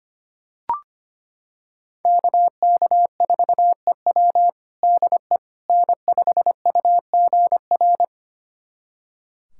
Morse code, heard as a keyed tone, about 25 wpm, 700 Hz. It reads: KK4EW DE N5UGR
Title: USA Callsign DE Callsign - 5 Characters - 25wpm